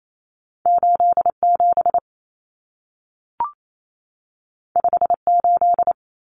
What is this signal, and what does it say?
Morse code, heard as a keyed tone, about 28 wpm, 700 Hz. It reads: 87 58
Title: Numbers - 2-Digit 28wpm